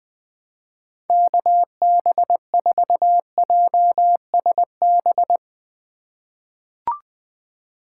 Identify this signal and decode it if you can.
Morse code, heard as a keyed tone, about 20 wpm, 700 Hz. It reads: KB4JSB
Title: USA Call Signs - 6 Characters - 4x Character Spacing - 20wpm